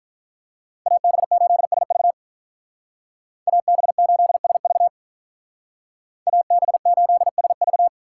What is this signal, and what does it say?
Morse code, heard as a keyed tone, about 44 wpm, 700 Hz. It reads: AB8SV AB8SV AB8SV